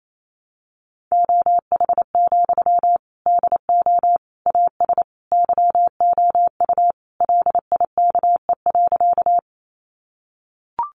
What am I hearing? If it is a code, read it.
Morse code, heard as a keyed tone, about 28 wpm, 700 Hz. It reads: OH, DO AS YOU LIKE.
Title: Sentences from Top 500 Words 28wpm